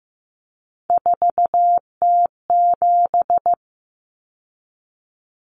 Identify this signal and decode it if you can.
Morse code, heard as a keyed tone, about 15 wpm, 700 Hz. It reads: 4T7
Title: Common Cut Numbers - 3-Digit 15wpm